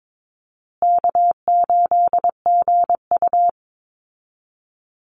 Morse code, heard as a keyed tone, about 22 wpm, 700 Hz. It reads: K8GU
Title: Easy Call Signs Repeated 3x - 22wpm